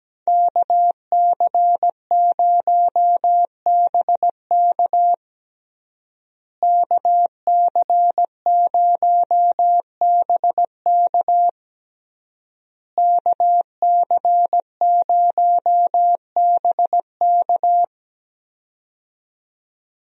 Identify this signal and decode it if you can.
Morse code, heard as a keyed tone, about 17 wpm, 700 Hz. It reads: KC0BK KC0BK KC0BK
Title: USA Call Signs - 5 Characters - Repeated 3x - 17wpm